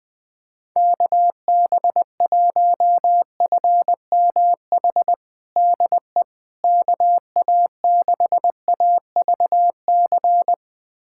Morse code, heard as a keyed tone, about 20 wpm, 700 Hz. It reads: KB1FMH DE KA6AVC
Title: USA Callsign DE Callsign - 6 Characters - 20wpm